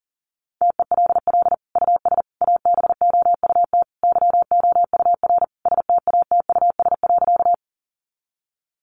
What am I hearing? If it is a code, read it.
Morse code, heard as a keyed tone, about 37 wpm, 700 Hz. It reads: TELL US ABOUT YOUR STATUS.
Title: Sentences from Top 1400 Words 40wpm